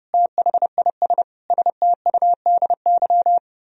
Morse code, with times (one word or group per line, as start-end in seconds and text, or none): 0.1-1.2 THIS
1.5-3.4 STUDY